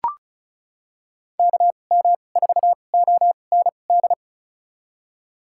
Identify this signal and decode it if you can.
Morse code, heard as a keyed tone, about 34 wpm, spaced as for 25 wpm, 700 Hz. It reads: KM4OND
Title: USA Call Signs - 6 Characters - 2x Character Spacing - 35wpm